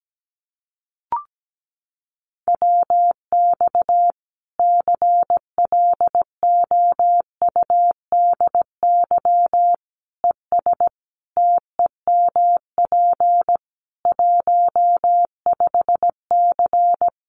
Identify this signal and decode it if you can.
Morse code, heard as a keyed tone, about 17 wpm, 700 Hz. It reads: WX CLOUDY ES TEMP 15C
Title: Common QSO Phrases 17wpm